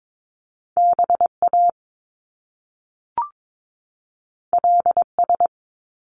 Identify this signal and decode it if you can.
Morse code, heard as a keyed tone, about 22 wpm, 700 Hz. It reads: BA LS
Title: Top Two Letter Combinations in Words 22wpm